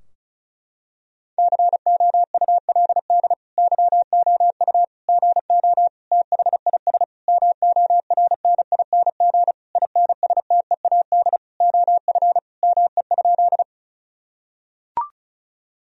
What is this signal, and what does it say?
Morse code, heard as a keyed tone, about 36 wpm, 700 Hz. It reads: COULD YOU GO THIS MORNING INSTEAD OF ME?
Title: Sentences from Top 1000 Words - Review Part 4 - 35wpm